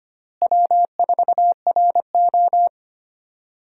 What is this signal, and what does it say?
Morse code, heard as a keyed tone, about 25 wpm, 700 Hz. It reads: W4RO
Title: USA Call Signs - 4 Characters - Repeated 2x - 25wpm